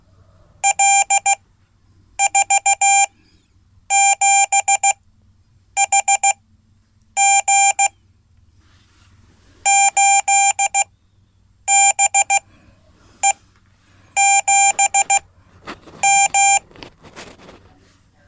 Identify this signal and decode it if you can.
Morse code, heard as a keyed tone, about 15 wpm, spaced as for 8 wpm, 790 Hz. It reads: L47HG 8BE7M
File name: Code / 5min-b